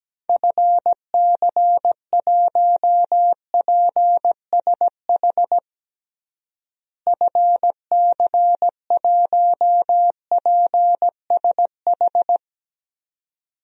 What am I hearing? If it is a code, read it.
Morse code, heard as a keyed tone, about 17 wpm, 700 Hz. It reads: FC1PSH FC1PSH
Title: DXCC-100 Call Signs - 6 Characters - Repeated 2x - 17wpm